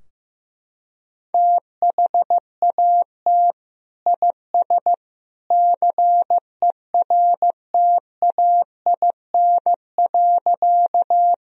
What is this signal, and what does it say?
Morse code, heard as a keyed tone, about 14 wpm, 700 Hz. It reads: THAT IS CERTAIN.
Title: Sentences from Top 1000 Words - Review Part 2 - 15wpm